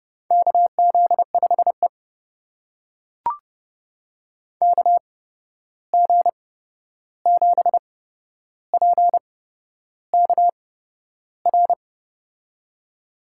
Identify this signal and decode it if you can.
Morse code, heard as a keyed tone, about 30 wpm, spaced as for 11 wpm, 700 Hz. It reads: KZ5E KG7PKR
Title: USA Call Signs - 8x Character Spacing - 30wpm